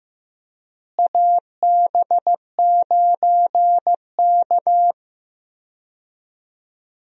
Morse code, heard as a keyed tone, about 15 wpm, 700 Hz. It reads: AB9K
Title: USA Call Signs - 4 Characters - Repeated 2x - 15wpm